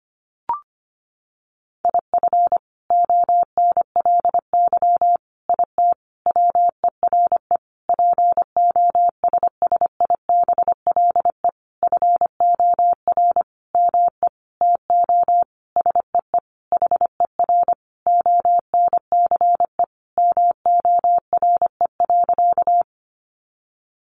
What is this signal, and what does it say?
Morse code, heard as a keyed tone, about 25 wpm, 700 Hz. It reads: IF ONLY IT WERE POSSIBLE FOR ME TO SEE HER ONCE MORE.
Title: Sentences from Top 500 Words 25wpm